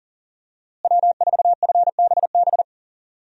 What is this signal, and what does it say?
Morse code, heard as a keyed tone, about 40 wpm, 700 Hz. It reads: W4FBB
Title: USA Call Signs - 5 Characters - Repeated 2x - 40wpm